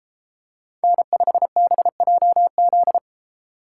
Morse code, heard as a keyed tone, about 32 wpm, 700 Hz. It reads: N5BJZ